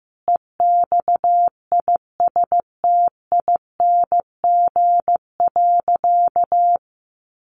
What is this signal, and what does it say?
Morse code, heard as a keyed tone, about 15 wpm, 700 Hz. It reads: EXISTING.
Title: Sentences from Top 1500 Words 15wpm